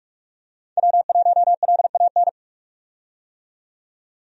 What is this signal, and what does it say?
Morse code, heard as a keyed tone, about 46 wpm, 700 Hz. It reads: W1LAN
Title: USA Call Signs - 5 Characters - 45wpm